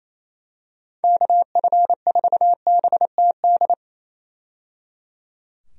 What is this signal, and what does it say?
Morse code, heard as a keyed tone, about 27 wpm, 700 Hz. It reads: KF4BTD
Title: USA Call Signs - 6 Characters - 28wpm